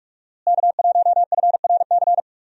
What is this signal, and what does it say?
Morse code, heard as a keyed tone, about 42 wpm, 700 Hz. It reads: K1FRC
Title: USA Call Signs - 5 Characters - Repeated 2x - 45wpm